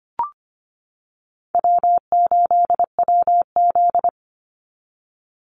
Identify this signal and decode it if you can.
Morse code, heard as a keyed tone, about 25 wpm, 700 Hz. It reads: W8WZ
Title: Call Signs - 25wpm